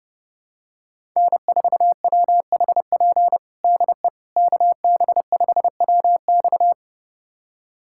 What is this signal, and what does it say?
Morse code, heard as a keyed tone, about 30 wpm, 700 Hz. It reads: N4WHP DE KB5WX
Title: USA Callsign DE Callsign - 5 Characters - 30wpm